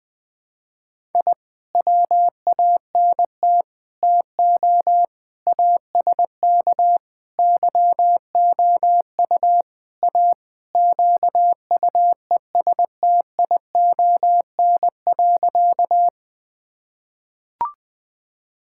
Morse code, heard as a keyed tone, about 20 wpm, 700 Hz. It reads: I WANT TO ASK YOU A QUESTION.
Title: Sentences with 5-Word Phrases - 20wpm